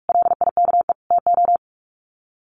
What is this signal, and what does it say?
Morse code, heard as a keyed tone, about 44 wpm, 700 Hz. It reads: LIKE TO
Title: Sets of 3 Words 45wpm